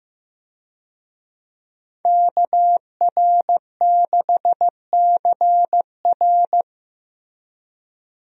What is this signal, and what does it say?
Morse code, heard as a keyed tone, about 15 wpm, 700 Hz. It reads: KR6CR
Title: USA Call Signs - 5 Characters - Repeated 3x - 15wpm